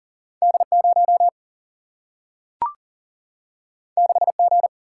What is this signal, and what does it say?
Morse code, heard as a keyed tone, about 42 wpm, 700 Hz. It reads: D 0 6 G
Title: Single-Letter Single-Number 40wpm